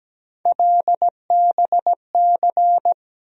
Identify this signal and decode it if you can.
Morse code, heard as a keyed tone, about 17 wpm, 700 Hz. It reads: LBC
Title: USA Call Sign Postfixes - 3 Letters 17wpm